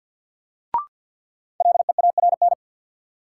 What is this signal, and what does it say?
Morse code, heard as a keyed tone, about 53 wpm, 700 Hz. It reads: LEARN
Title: Sentences from Top 300 Words - New Word Drill - 50wpm